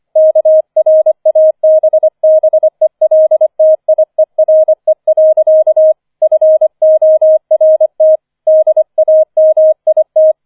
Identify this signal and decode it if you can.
Morse code, heard as a keyed tone, about 24 wpm, 630 Hz. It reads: KRABBELTIERE. FORT DAMIT
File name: 005 Fünfter Streich 25wpm